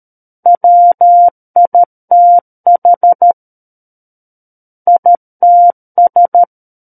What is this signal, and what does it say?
Morse code, heard as a keyed tone, about 13 wpm, 700 Hz. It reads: WITH ITS
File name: Book One / 04-Chapter 4 - 13 WPM